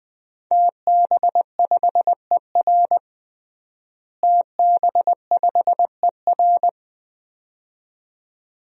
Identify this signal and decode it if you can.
Morse code, heard as a keyed tone, about 20 wpm, 700 Hz. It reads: TB5ER TB5ER
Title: DXCC-100 Call Signs - 5 Characters - Repeated 2x - 20wpm